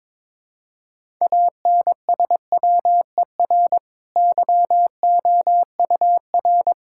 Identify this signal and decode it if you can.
Morse code, heard as a keyed tone, about 22 wpm, 700 Hz. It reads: ANSWER YOUR
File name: practice / 22wpm / Easy Sentences with Hints - 22wpm